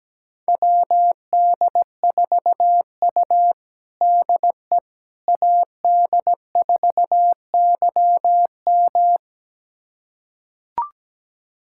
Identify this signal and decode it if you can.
Morse code, heard as a keyed tone, about 17 wpm, 700 Hz. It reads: WD4U DE AD4YM
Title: USA Callsign DE Callsign 17wpm